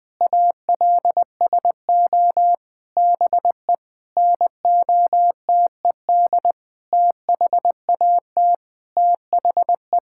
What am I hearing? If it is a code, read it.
Morse code, heard as a keyed tone, about 20 wpm, 700 Hz. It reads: ALSO BE NOTED THAT THE
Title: Sentences with 7-Word Phrases - Spoken Beginning - 20wpm